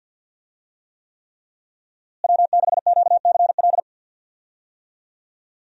Morse code, heard as a keyed tone, about 46 wpm, 700 Hz. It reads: W6QCL
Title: USA Call Signs - 5 Characters - Repeated 3x - 50wpm